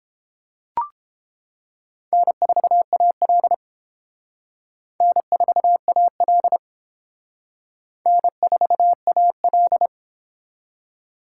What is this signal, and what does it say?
Morse code, heard as a keyed tone, about 29 wpm, 700 Hz. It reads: N4AL N4AL N4AL